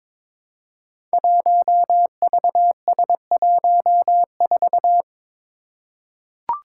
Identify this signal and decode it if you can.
Morse code, heard as a keyed tone, about 22 wpm, 700 Hz. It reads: 1VS14
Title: Five Challenging Letter-Number Combinations 22wpm